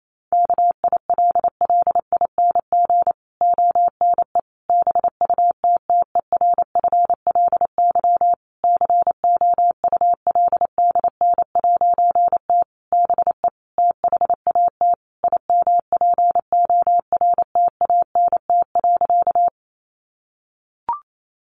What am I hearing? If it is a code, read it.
Morse code, heard as a keyed tone, about 28 wpm, 700 Hz. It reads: KILLING ONE BUTTERFLY COULDN'T BE THAT IMPORTANT.
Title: A Sound Of Thunder 28wpm